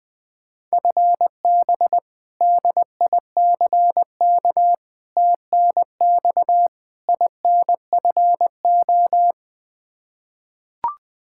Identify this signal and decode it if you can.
Morse code, heard as a keyed tone, about 20 wpm, 700 Hz. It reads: FB DICK TNX INFO
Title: Practice QSO - Exchange 3 - More Info - 20wpm